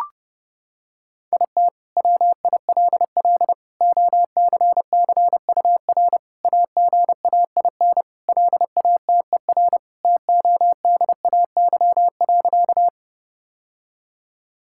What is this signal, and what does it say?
Morse code, heard as a keyed tone, about 30 wpm, 700 Hz. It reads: IT WILL OCCUR AGAIN LATER TODAY.
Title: Sentences from Top 1000 Words - Review Part 6 - 30wpm